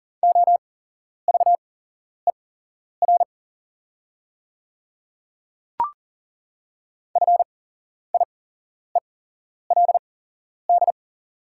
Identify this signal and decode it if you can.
Morse code, heard as a keyed tone, about 40 wpm, spaced as for 11 wpm, 690 Hz. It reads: OVER FIELD
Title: 8x Character Spacing - Top 500 Words - 40wpm